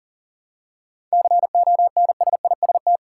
Morse code, heard as a keyed tone, about 40 wpm, 700 Hz. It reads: CONSIST
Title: Seven Letter Words - 40wpm